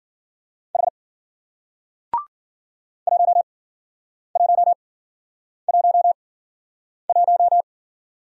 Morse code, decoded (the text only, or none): H 1111